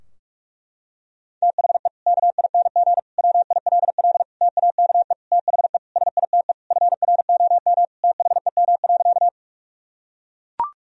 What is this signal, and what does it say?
Morse code, heard as a keyed tone, about 41 wpm, 700 Hz. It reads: THE KING WILL TAKE THE SITE FROM THEM.
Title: Sentences from Top 1000 Words - Review Part 4 - 45wpm